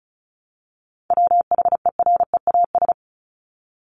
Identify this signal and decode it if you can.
Morse code, heard as a keyed tone, about 34 wpm, 700 Hz. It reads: WHEREAS
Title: Top 1501-1750 Words 35wpm